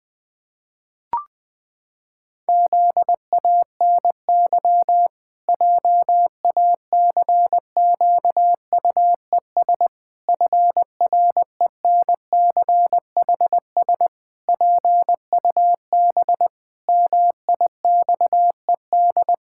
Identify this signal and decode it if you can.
Morse code, heard as a keyed tone, about 21 wpm, 700 Hz. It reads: ZANY JACQUES FRENCHS PUB MIXED
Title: Pangram Challenge v2 - 20wpm